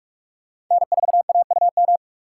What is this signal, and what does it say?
Morse code, heard as a keyed tone, about 46 wpm, 700 Hz. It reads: N4AUM
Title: USA Call Signs - 5 Characters - 4x Character Spacing - 45wpm